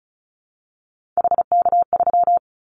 Morse code, heard as a keyed tone, about 36 wpm, 700 Hz. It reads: HK3
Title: Callsign Prefixes for Top 100 DXCC - 35wpm